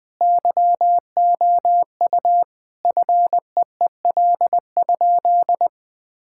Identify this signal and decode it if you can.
Morse code, heard as a keyed tone, about 20 wpm, 700 Hz. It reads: YOU FEEL?
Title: Sentences with 3-Word Phrases - Spoken Beginning - 20wpm